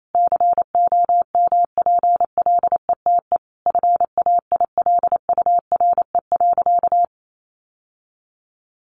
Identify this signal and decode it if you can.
Morse code, heard as a keyed tone, about 28 wpm, 700 Hz. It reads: COMPLETE FAILURE.
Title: Sentences from Top 1400 Words 28wpm